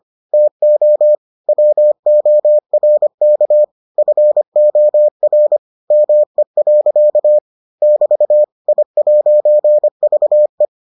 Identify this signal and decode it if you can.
Morse code, heard as a keyed tone, about 25 wpm, 600 Hz. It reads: TO WORK FOR ME. <BT> I'VE